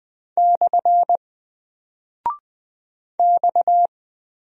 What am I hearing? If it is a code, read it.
Morse code, heard as a keyed tone, about 20 wpm, 700 Hz. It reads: / X
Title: Single Character (Letter:Number:Punctuation:Prosign) 20wpm